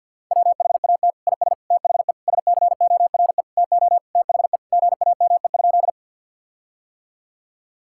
Morse code, heard as a keyed tone, about 50 wpm, 700 Hz. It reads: WHAT IS THE SCORE TO THE GAME?
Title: Sentences from Top 1600 Words 50wpm